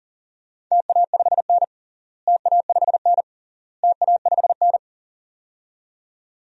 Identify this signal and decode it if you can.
Morse code, heard as a keyed tone, about 40 wpm, 700 Hz. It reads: TA5N TA5N TA5N
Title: DXCC-100 Call Signs - 4 Characters - Repeated 3x - 40wpm